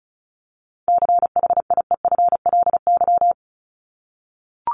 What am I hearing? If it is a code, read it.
Morse code, heard as a keyed tone, about 34 wpm, 700 Hz. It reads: CHIEFLY
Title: LY Suffix - Top 100 Words with Base Words 35wpm